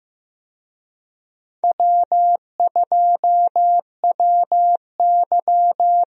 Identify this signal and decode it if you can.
Morse code, heard as a keyed tone, about 15 wpm, 700 Hz. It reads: W2WY
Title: USA Call Signs - 4 Characters - Repeated 3x - 15wpm